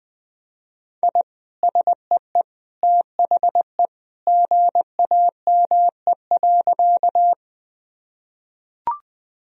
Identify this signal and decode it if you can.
Morse code, heard as a keyed tone, about 20 wpm, 700 Hz. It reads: I SEE THE GAME.
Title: Sentences from Top 1000 Words - Review Part 3 - 20wpm